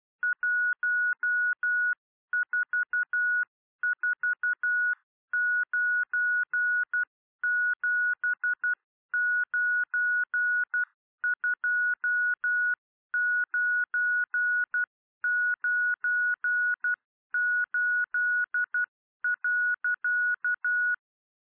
Morse code, heard as a keyed tone, about 12 wpm, 1460 Hz. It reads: 1449792998.
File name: radiocap / REC z